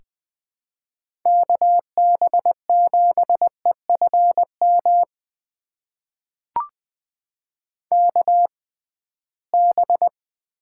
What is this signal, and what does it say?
Morse code, heard as a keyed tone, about 20 wpm, spaced as for 15 wpm, 700 Hz. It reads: KB7EFM KB
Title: USA Call Signs - 6 Characters - 6x Character Spacing - 20wpm